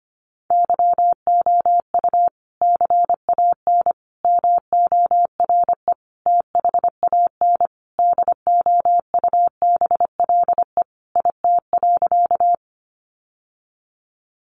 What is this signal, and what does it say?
Morse code, heard as a keyed tone, about 25 wpm, 700 Hz. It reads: YOU CAN MORE THAN DOUBLE IT.
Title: Sentences from Top 1400 Words 25wpm